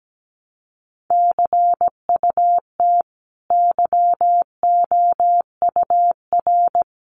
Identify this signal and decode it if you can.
Morse code, heard as a keyed tone, about 17 wpm, 700 Hz. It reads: CUT YOUR
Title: Sentences from Top 300 Words 17wpm